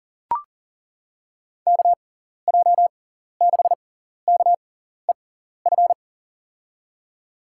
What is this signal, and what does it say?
Morse code, heard as a keyed tone, about 40 wpm, spaced as for 14 wpm, 700 Hz. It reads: KJ6KEF
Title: USA Call Signs - 6 Characters - 6x Character Spacing - 40wpm